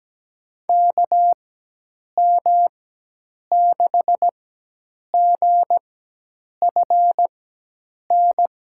Morse code, heard as a keyed tone, about 17 wpm, spaced as for 8 wpm, 700 Hz. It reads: KM6GFN
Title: USA Call Signs - 6 Characters - 4x Character Spacing - 17wpm